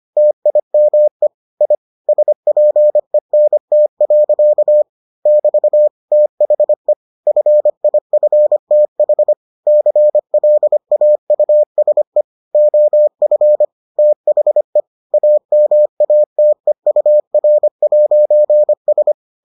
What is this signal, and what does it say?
Morse code, heard as a keyed tone, about 25 wpm, 600 Hz. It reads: TIME I SPENT. <BT> THE FIFTH CLAUSE OF THE AMATEUR'S